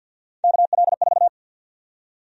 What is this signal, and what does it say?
Morse code, heard as a keyed tone, about 53 wpm, 700 Hz. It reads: KL4
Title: USA Call Sign Prefixes - 50wpm